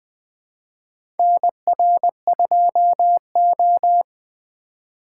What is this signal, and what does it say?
Morse code, heard as a keyed tone, about 20 wpm, 700 Hz. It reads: NR2O
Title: USA Call Signs - 4 Characters - 6x Character Spacing - 20wpm